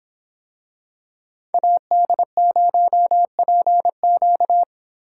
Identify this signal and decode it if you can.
Morse code, heard as a keyed tone, about 25 wpm, 700 Hz. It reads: AD0PQ